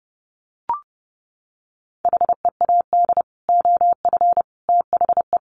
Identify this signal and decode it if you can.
Morse code, heard as a keyed tone, about 30 wpm, 700 Hz. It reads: HEAD OF THE
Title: Sets of 3 Words 30wpm